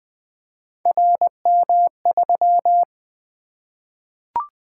Morse code, heard as a keyed tone, about 20 wpm, 700 Hz. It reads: RM3
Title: Three Letter-Number Combinations - 20wpm